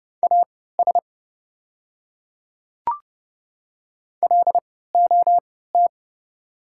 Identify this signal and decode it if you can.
Morse code, heard as a keyed tone, about 30 wpm, spaced as for 17 wpm, 690 Hz. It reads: AS LOT
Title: 3x Character Spacing - Top 500 Words - 30wpm